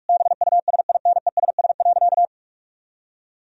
Morse code, heard as a keyed tone, about 46 wpm, 700 Hz. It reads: BUSINESS.
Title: Sentences from Top 1000 Words - Review Part 4 - 45wpm